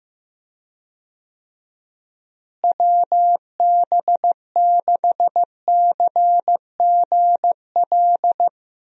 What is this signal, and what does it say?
Morse code, heard as a keyed tone, about 15 wpm, 700 Hz. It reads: WB6CGL
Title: USA Call Signs - 6 Characters - Repeated 2x - 15wpm